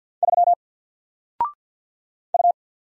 Morse code, heard as a keyed tone, about 49 wpm, 700 Hz. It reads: 3 U
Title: Single Character (Letter:Number:Punctuation:Prosign) 50wpm